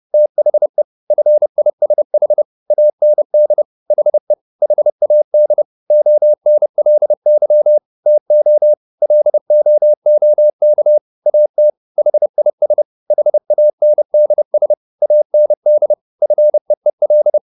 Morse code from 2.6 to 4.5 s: AND HE

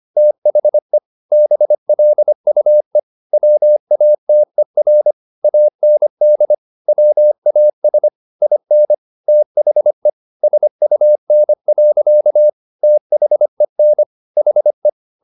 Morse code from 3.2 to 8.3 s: WATER AND WAS

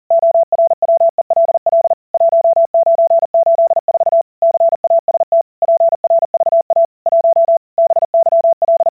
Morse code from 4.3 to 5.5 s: CAST

